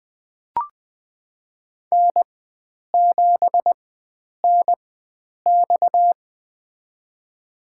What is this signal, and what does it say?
Morse code, heard as a keyed tone, about 20 wpm, spaced as for 9 wpm, 700 Hz. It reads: N7NX
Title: USA Call Signs - 4x Character Spacing - 20wpm